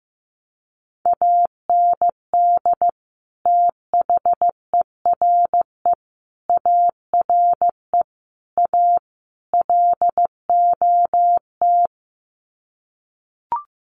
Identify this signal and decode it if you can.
Morse code, heard as a keyed tone, about 15 wpm, 700 Hz. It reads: AND THERE ARE A LOT
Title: Sets of 5 Words Encore2 15wpm